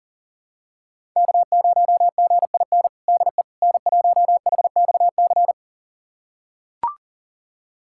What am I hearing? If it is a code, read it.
Morse code, heard as a keyed tone, about 40 wpm, 700 Hz. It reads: K0GIN DE N1HXC
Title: USA Callsign DE Callsign - 5 Characters - 40wpm